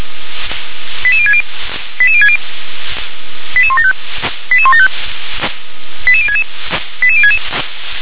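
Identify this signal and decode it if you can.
Five-tone selective call (ZVEI), heard as quick runs of five tones, tones of about 70 ms. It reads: 88060 88060 88175 88175 88066 88066
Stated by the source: ZVEI five-tone call received on SDR.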